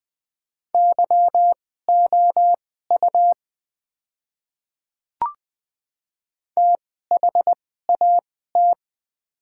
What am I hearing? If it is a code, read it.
Morse code, heard as a keyed tone, about 20 wpm, spaced as for 14 wpm, 700 Hz. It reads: YOU THAT
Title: 2x Character Spacing - Top 100 Words - 20wpm